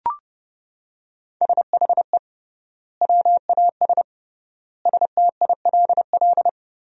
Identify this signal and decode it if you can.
Morse code, heard as a keyed tone, about 32 wpm, 700 Hz. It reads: SHE WAS STILL
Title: 3x Word-Spacing - Sentences from Top 500 Words - 30wpm